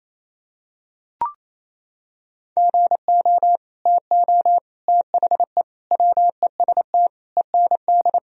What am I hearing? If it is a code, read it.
Morse code, heard as a keyed tone, about 27 wpm, 700 Hz. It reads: GO TO THE WEST END
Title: Sentences from Top 1000 Words - Review Part 2 - 28wpm